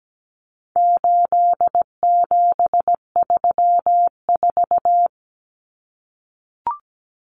Morse code, heard as a keyed tone, about 17 wpm, 700 Hz. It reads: 8734
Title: Common Cut Numbers - 4-Digit 17wpm